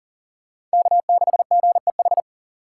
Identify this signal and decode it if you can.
Morse code, heard as a keyed tone, about 40 wpm, 700 Hz. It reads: K6GEH
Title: USA Call Signs - 5 Characters - 2x Character Spacing - 40wpm